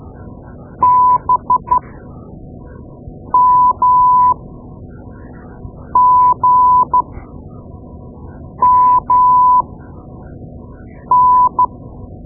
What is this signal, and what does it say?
Morse code, heard as a keyed tone, about 13 wpm, spaced as for 5 wpm, 1000 Hz. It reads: BMGMN
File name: Morse Lessons / Morse Lessons / Tape-2B